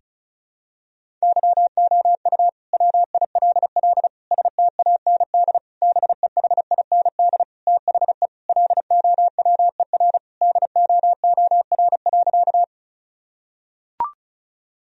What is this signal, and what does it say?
Morse code, heard as a keyed tone, about 36 wpm, 700 Hz. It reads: YOU WILL STAND BEHIND THE LOWER DOOR.
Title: Sentences from Top 1000 Words - Review Part 4 - 35wpm